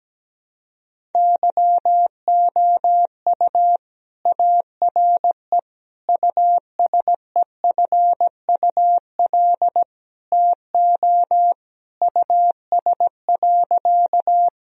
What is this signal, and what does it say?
Morse code, heard as a keyed tone, about 17 wpm, 700 Hz. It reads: YOU ARE USEFUL TO US.
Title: Sentences from Top 1000 Words - Review Part 4 - 17wpm